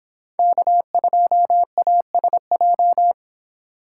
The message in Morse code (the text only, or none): K2ASJ